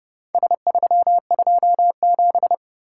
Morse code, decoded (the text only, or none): S327